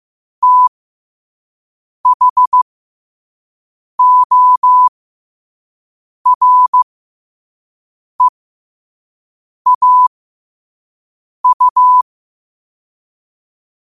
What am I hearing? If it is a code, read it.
Morse code, heard as a keyed tone, about 15 wpm, spaced as for 5 wpm, 1000 Hz. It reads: THOREAU
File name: qotdfiles / 5wpm